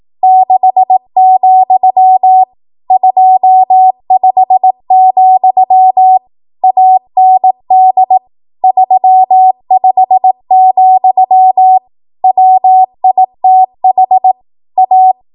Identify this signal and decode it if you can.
Morse code, heard as a keyed tone, about 18 wpm, 750 Hz. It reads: 6, 25, AND 35, WITH A
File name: Morse / 18WPM